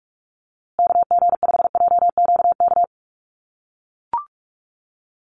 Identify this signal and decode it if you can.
Morse code, heard as a keyed tone, about 44 wpm, 700 Hz. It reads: KG5JQK
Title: USA Call Signs Repeated 2x 45wpm